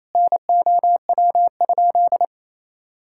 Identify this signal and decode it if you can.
Morse code, heard as a keyed tone, about 27 wpm, 700 Hz. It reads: NOW?
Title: Sentences from Top 1000 Words - Review Part 5 - 28wpm